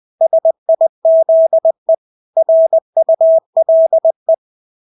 Morse code, heard as a keyed tone, about 20 wpm, 650 Hz. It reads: SIZE RULE
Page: Engelske ord 20 wpm